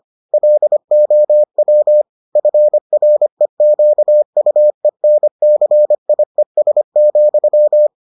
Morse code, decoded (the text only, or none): LOW FREQUENCIES,